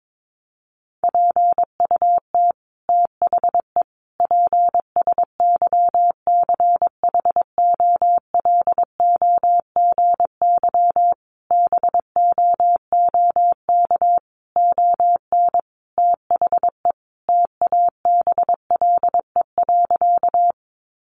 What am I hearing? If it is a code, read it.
Morse code, heard as a keyed tone, about 22 wpm, 700 Hz. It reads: PUT THE PSYCHOLOGY BOOK ON THE TABLE.
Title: Sentences from Top 1400 Words 22wpm